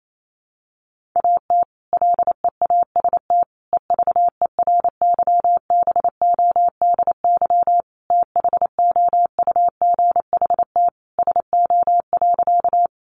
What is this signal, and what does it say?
Morse code, heard as a keyed tone, about 28 wpm, 700 Hz. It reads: AT LEAST EVERYBODY THOUGHT SO.
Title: Sentences from Top 1400 Words 28wpm